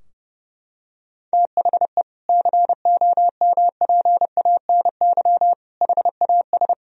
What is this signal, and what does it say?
Morse code, heard as a keyed tone, about 30 wpm, 700 Hz. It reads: THE COMPANY HAS
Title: Sentences from Top 1000 Words - Review Part 3 - 30wpm